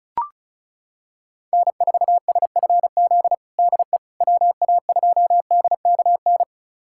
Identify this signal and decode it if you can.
Morse code, heard as a keyed tone, about 36 wpm, 700 Hz. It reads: N4SFZ DE WA2DKN
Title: USA Callsign DE Callsign 35wpm